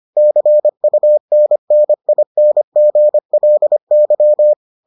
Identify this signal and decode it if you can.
Morse code, heard as a keyed tone, about 25 wpm, 600 Hz. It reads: CUNNINGLY